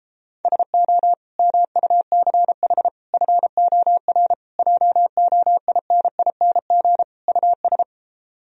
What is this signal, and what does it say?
Morse code, heard as a keyed tone, about 33 wpm, 700 Hz. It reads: SO MUCH FOR JOINING US